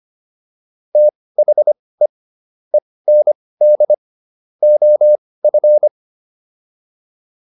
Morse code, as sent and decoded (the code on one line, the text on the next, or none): - .... / . -. -.. / --- ..-.
THE END OF